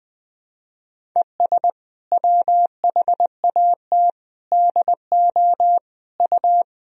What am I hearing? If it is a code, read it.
Morse code, heard as a keyed tone, about 20 wpm, 700 Hz. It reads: ES WHAT DO U
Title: Sentences from Top 100 Words and Abbreviations 20wpm